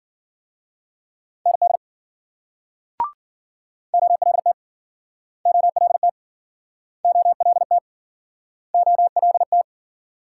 Morse code, heard as a keyed tone, about 53 wpm, 700 Hz. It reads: ND OLT OLT OLT OLT